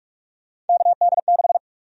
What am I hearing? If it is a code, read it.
Morse code, heard as a keyed tone, about 46 wpm, 700 Hz. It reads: KD6
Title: USA Call Sign Prefixes - 45wpm